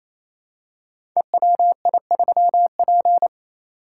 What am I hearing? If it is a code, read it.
Morse code, heard as a keyed tone, about 29 wpm, 700 Hz. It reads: EWI3P
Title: Code Groups - 5 Letter:Numbers - 28wpm